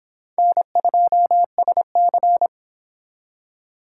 Morse code, heard as a keyed tone, about 26 wpm, 700 Hz. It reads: N2SC